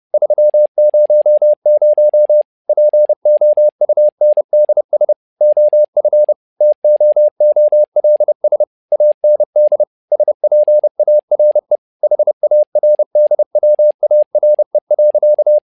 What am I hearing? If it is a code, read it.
Morse code, heard as a keyed tone, about 30 wpm, 600 Hz. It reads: 300 POUNDS OF TOOLS AND SPARE HARDWARE.
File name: learning center / tower trouble / tower trouble 30wpm